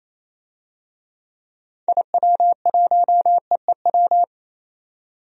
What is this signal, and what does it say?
Morse code, heard as a keyed tone, about 28 wpm, 700 Hz. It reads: IW1EEW
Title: DXCC-100 Call Signs - 6 Characters - 28wpm